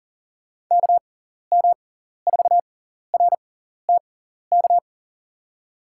Morse code, heard as a keyed tone, about 40 wpm, spaced as for 14 wpm, 700 Hz. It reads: KM4RTK
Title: USA Call Signs - 6 Characters - 6x Character Spacing - 40wpm